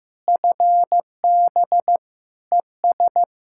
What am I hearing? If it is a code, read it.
Morse code, heard as a keyed tone, about 15 wpm, 700 Hz. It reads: FB ES
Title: K1USN SST QSO Practice 15wpm